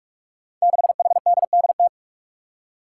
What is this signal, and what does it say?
Morse code, heard as a keyed tone, about 44 wpm, 700 Hz. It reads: 6HDDT
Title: Five Challenging Letter-Number Combinations 45wpm